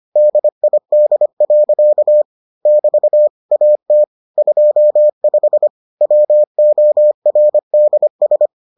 Morse code, heard as a keyed tone, about 25 wpm, 600 Hz. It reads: DID. <BT> AT 25 WORDS